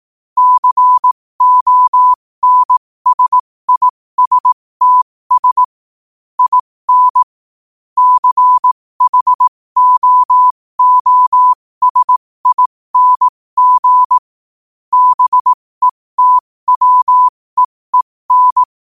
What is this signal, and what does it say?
Morse code, heard as a keyed tone, about 18 wpm, spaced as for 15 wpm, 1000 Hz. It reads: CONSISTS IN CHOOSING BETWEEN
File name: qotdfiles / 15wpm